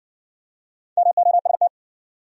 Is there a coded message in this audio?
Yes: Morse code from 1.0 to 1.7 s, heard as a keyed tone, about 60 wpm, 700 Hz. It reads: MOST